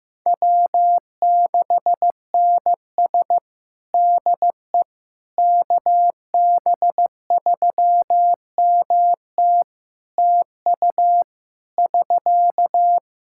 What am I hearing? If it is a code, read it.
Morse code, heard as a keyed tone, about 15 wpm, 700 Hz. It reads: W6NS DE KB3MT TU <SK>
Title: Practice QSO - Exchange 4 - Ending - 15wpm